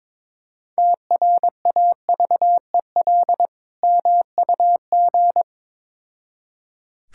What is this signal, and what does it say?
Morse code, heard as a keyed tone, about 22 wpm, 700 Hz. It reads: TRAVEL MUG
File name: Difficult Words with Hints - 22wpm